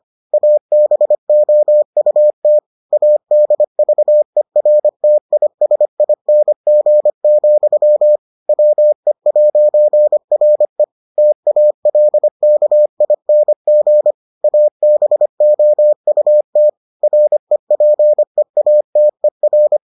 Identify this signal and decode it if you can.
Morse code, heard as a keyed tone, about 25 wpm, 600 Hz. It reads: ABOUT ADVERTISING, WE'RE TALKING ABOUT REPEATER